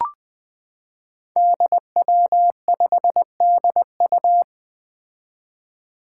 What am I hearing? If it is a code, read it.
Morse code, heard as a keyed tone, about 20 wpm, 700 Hz. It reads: DW5DU
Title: Five Challenging Letter-Number Combinations 20wpm